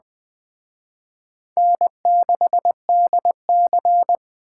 Morse code, heard as a keyed tone, about 21 wpm, 700 Hz. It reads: N6DC
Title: Speed Racing - USA 1x2 Call Signs 17wpm